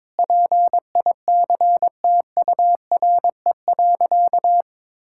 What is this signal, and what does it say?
Morse code, heard as a keyed tone, about 22 wpm, 700 Hz. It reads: PICTURE.
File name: practice / 22wpm / Sentences with 8-Word Phrases - 22wpm